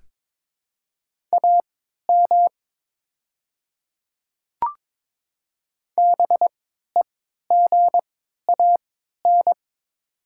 Morse code, heard as a keyed tone, about 22 wpm, spaced as for 12 wpm, 700 Hz. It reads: AM BEGAN
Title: 3x Character Spacing - Top 500 Words - 22wpm